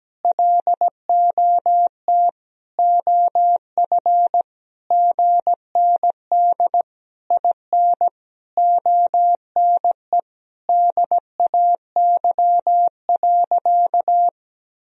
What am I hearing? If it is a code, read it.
Morse code, heard as a keyed tone, about 17 wpm, 700 Hz. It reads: LOT OF GND IN ONE DAY.
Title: Sentences from Top 500 Words and Abbreviations 17wpm